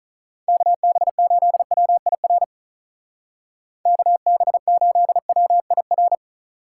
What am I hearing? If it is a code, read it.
Morse code, heard as a keyed tone, about 39 wpm, 700 Hz. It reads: KB8WIR KB8WIR
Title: Speed Racing - USA 2x3 Call Signs 35wpm